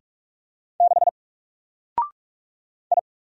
Morse code, heard as a keyed tone, about 44 wpm, 700 Hz. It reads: 6 I
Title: Single Character (Letter:Number:Punctuation:Prosign) Alt 45wpm